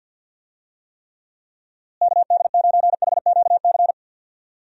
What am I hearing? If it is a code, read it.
Morse code, heard as a keyed tone, about 52 wpm, 700 Hz. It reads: KD9HQC
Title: USA Call Signs - 6 Characters - Repeated 3x - 50wpm